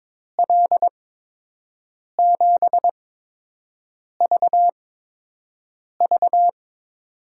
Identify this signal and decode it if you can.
Morse code, heard as a keyed tone, about 22 wpm, spaced as for 6 wpm, 700 Hz. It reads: L7VV
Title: USA Call Signs - 8x Character Spacing - 22wpm